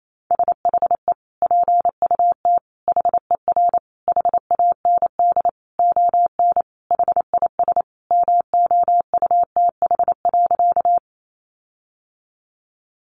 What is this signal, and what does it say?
Morse code, heard as a keyed tone, about 28 wpm, 700 Hz. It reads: SHE PUT HER HAND ON HIS MOUTH.
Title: Sentences from Top 1500 Words 28wpm